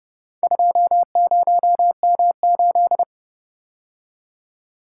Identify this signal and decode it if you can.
Morse code, heard as a keyed tone, about 29 wpm, 700 Hz. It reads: 20M8